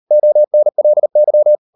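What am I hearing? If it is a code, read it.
Morse code, heard as a keyed tone, about 38 wpm, 600 Hz. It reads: ONLY
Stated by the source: Morse code practice at 38 WPM, sending Ken Follett's novel Never in English, from CW med Gnister.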